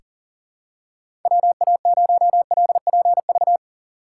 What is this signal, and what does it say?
Morse code, heard as a keyed tone, about 40 wpm, 700 Hz. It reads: WA0LPV
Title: K1USN SST QSO Practice - 40wpm